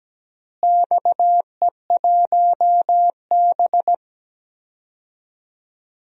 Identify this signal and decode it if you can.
Morse code, heard as a keyed tone, about 17 wpm, 700 Hz. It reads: XE1B
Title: DXCC-100 Call Signs - 4 Characters - Repeated 2x - 17wpm